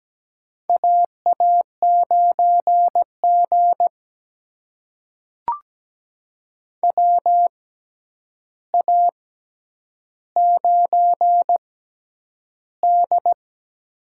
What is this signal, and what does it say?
Morse code, heard as a keyed tone, about 17 wpm, spaced as for 9 wpm, 700 Hz. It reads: AA9G WA9D
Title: USA Call Signs - 4 Characters - 6x Character Spacing - 17wpm